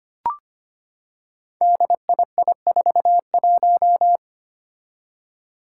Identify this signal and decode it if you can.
Morse code, heard as a keyed tone, about 25 wpm, 700 Hz. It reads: DII41
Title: Five Challenging Letter-Number Combinations 25wpm